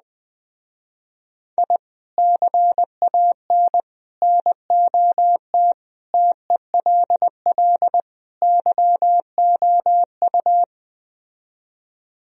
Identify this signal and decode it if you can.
Morse code, heard as a keyed tone, about 20 wpm, 700 Hz. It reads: I CAN NOT TELL YOU
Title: Speed-Racing - Sets of 5 Words 17wpm